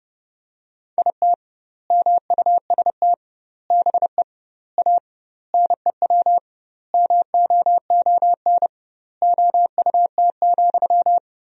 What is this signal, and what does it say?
Morse code, heard as a keyed tone, about 30 wpm, 700 Hz. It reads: IT MUST BE A NEW MOON OUT,
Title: 2x Word-Spacing - Sentences from Top 500 Words - 30wpm